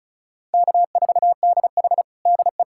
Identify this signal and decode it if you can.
Morse code, heard as a keyed tone, about 34 wpm, 700 Hz. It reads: K4DH DE
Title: USA Callsign DE Callsign 35wpm